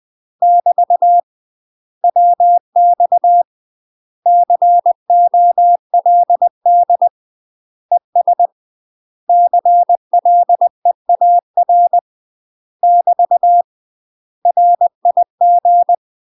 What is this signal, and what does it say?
Morse code, heard as a keyed tone, about 20 wpm, 700 Hz. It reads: <BT> WX COLD ES CLEAR <BT> RIG
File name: UK MASTER QSOs TRAINING / 20WPM / UK MASTER QSOs 0051